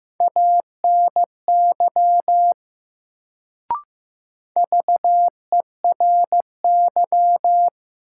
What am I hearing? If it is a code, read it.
Morse code, heard as a keyed tone, about 15 wpm, 700 Hz. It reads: ANY VERY
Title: Sentences from Top 200 Words - New Word Drill - 15wpm